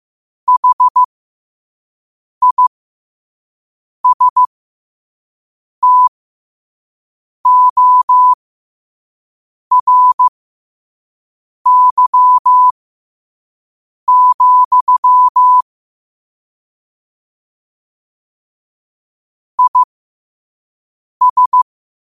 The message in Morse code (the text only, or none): HISTORY, IS